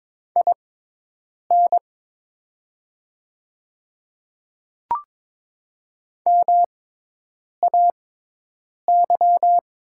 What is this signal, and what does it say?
Morse code, heard as a keyed tone, about 22 wpm, spaced as for 8 wpm, 700 Hz. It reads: IN MAY
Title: 6x Character Spacing - Top 100 Words - 22wpm